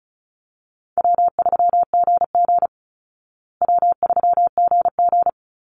Morse code, heard as a keyed tone, about 34 wpm, 700 Hz. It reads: W3GG W3GG
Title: Call Signs Repeated 3x - 35wpm